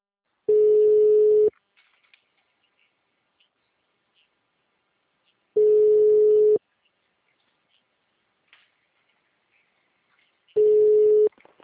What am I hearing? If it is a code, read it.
Ringback tone (Europe, 425 Hz).